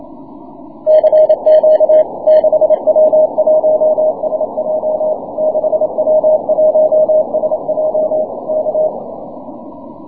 Morse code, heard as a keyed tone, about 27 wpm, 620 Hz. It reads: CQ 6W1SJ 6W1SJ K
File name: wp / 6w1sj 6m cw